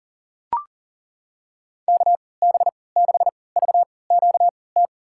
Morse code, heard as a keyed tone, about 40 wpm, spaced as for 23 wpm, 700 Hz. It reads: KB6VQT
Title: USA Call Signs - 6 Characters - 3x Character Spacing - 40wpm